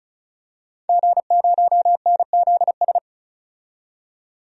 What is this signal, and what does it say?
Morse code, heard as a keyed tone, about 34 wpm, 700 Hz. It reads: G0NZS
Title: DXCC-100 Call Signs - 5 Characters - 35wpm